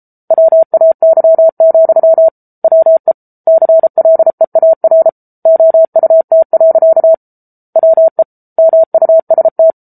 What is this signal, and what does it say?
Morse code, heard as a keyed tone, about 34 wpm, 650 Hz. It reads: WAY, WE CLEAR OUT. WE MUST
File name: Book Two / 07-Chapter 7 - 33 WPM